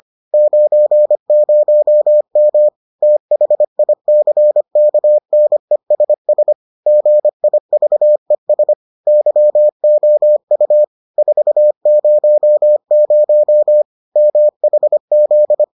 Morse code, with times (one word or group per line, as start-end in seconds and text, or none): 0.3-2.7 90M
3.0-6.5 THICKNESS
6.9-8.7 GIVES
9.1-10.8 YOU
11.2-13.8 400
14.2-15.6 MHZ